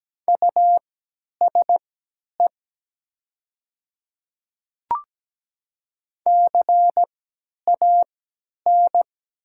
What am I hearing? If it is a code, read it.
Morse code, heard as a keyed tone, about 17 wpm, spaced as for 10 wpm, 700 Hz. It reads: USE CAN
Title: 3x Character Spacing - Top 500 Words - 17wpm